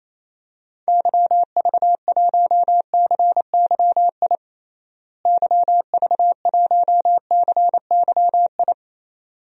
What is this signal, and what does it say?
Morse code, heard as a keyed tone, about 28 wpm, 700 Hz. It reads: YV1CYI YV1CYI
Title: DXCC-100 Call Signs - 6 Characters - Repeated 2x - 28wpm